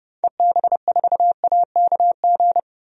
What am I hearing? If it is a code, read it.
Morse code, heard as a keyed tone, about 30 wpm, 690 Hz. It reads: EB4AKG
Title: DXCC-100 Call Signs - 6 Characters - 30wpm